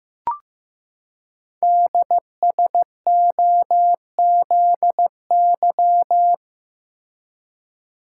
Morse code, heard as a keyed tone, about 15 wpm, 700 Hz. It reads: DSOZY
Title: Code Groups - 5 Letters - 15wpm